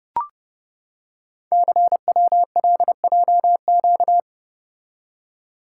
Morse code, heard as a keyed tone, about 30 wpm, 700 Hz. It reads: CWLJQ
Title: Code Groups - 5 Letters - 30wpm